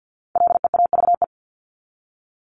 Morse code, heard as a keyed tone, about 50 wpm, 700 Hz. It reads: LEAVE
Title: Top 500 Words 50wpm